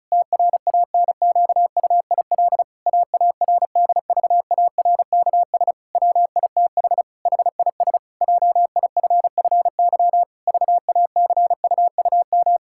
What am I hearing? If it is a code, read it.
Morse code, heard as a keyed tone, about 34 wpm, 700 Hz. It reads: TRANQUIL AARDVARKS WITH HIS JIFFY VACUUM
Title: Pangram Challenge v2 - 35wpm